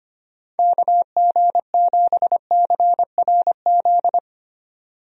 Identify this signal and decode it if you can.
Morse code, heard as a keyed tone, about 25 wpm, 700 Hz. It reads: KG7CRZ
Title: USA Call Signs - 6 Characters - 25wpm